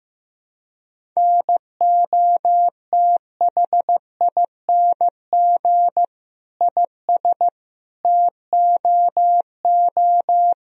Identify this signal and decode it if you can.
Morse code, heard as a keyed tone, about 16 wpm, 700 Hz. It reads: NOTHING IS TOO
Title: Sentences from Top 1000 Words - Review Part 1 - 15wpm